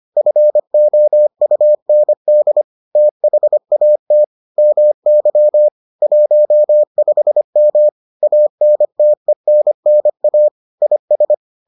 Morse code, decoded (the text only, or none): FOUND THAT MY 15M ANTENNA IS